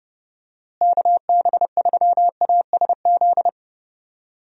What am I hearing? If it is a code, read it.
Morse code, heard as a keyed tone, about 30 wpm, 700 Hz. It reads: KB3ASZ
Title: USA Call Signs - 6 Characters - Repeated 2x - 30wpm